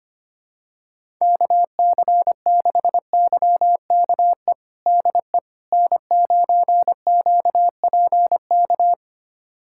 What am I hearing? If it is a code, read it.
Morse code, heard as a keyed tone, about 25 wpm, 700 Hz. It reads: KC6YKE DE N9QPK
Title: USA Callsign DE Callsign 25wpm